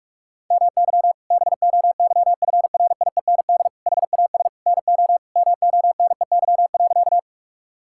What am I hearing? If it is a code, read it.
Morse code, heard as a keyed tone, about 46 wpm, 700 Hz. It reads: MY BOYFRIEND HAS NO MONEY.
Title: Sentences from Top 1000 Words - Review Part 5 - 45wpm